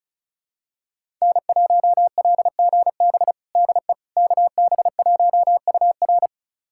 Morse code, heard as a keyed tone, about 36 wpm, 700 Hz. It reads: N1LGB DE KB1UR
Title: USA Callsign DE Callsign - 5 Characters - 35wpm